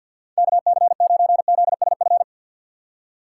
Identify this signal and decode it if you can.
Morse code, heard as a keyed tone, about 50 wpm, 700 Hz. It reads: KC9ZSF
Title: USA Call Signs - 6 Characters - 6x Character Spacing - 50wpm